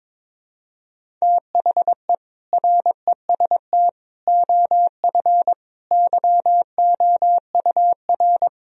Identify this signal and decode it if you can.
Morse code, heard as a keyed tone, about 22 wpm, 700 Hz. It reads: THE REST OF YOUR
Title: Speed-Racing - Sets of 5 Words 22wpm